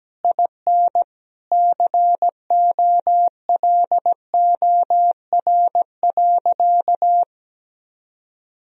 Morse code, heard as a keyed tone, about 17 wpm, 700 Hz. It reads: IN COLOR.
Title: Sentences from Top 1000 Words - Review Part 2 - 17wpm